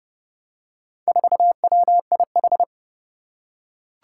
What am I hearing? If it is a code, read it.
Morse code, heard as a keyed tone, about 30 wpm, 700 Hz. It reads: 4WIH
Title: Four Challenging Letter-Number Combinations 30wpm